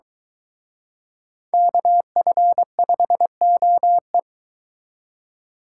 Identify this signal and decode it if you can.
Morse code, heard as a keyed tone, about 22 wpm, 700 Hz. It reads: KF5OE